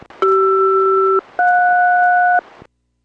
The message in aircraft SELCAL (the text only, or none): CR-JS